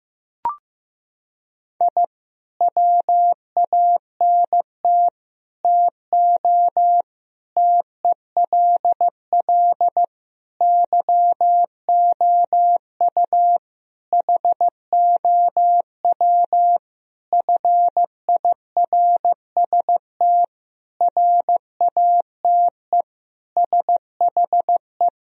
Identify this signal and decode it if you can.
Morse code, heard as a keyed tone, about 15 wpm, 700 Hz. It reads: I WANT TO TELL YOU HOW FIRST RATE SHE
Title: Sentences from Top 1000 Words - Review Part 4 - 15wpm